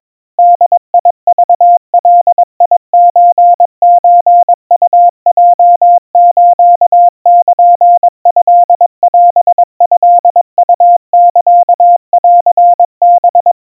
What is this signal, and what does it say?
Morse code (Japanese Wabun code), heard as a keyed tone, about 22 wpm, 700 Hz. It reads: ボクガソソウヲスルトオトウサンハ
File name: 80 cpm 700 Hz Crispy